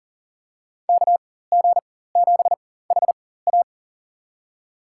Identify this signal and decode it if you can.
Morse code, heard as a keyed tone, about 37 wpm, spaced as for 18 wpm, 700 Hz. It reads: KG7HA
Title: USA Call Signs - 4x Character Spacing - 40wpm